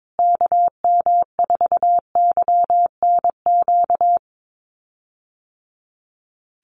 Morse code, heard as a keyed tone, about 22 wpm, 700 Hz. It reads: KM4YNQ
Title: USA Call Signs Repeated 2x 22wpm